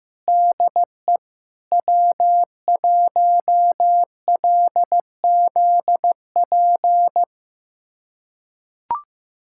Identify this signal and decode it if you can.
Morse code, heard as a keyed tone, about 15 wpm, 700 Hz. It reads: DE W1LZP
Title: USA Callsign DE Callsign 15wpm